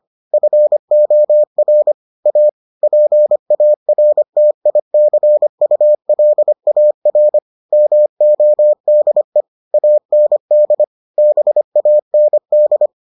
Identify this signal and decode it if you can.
Morse code, heard as a keyed tone, about 25 wpm, 600 Hz. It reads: FOR A PARTICULAR MODE AND BAND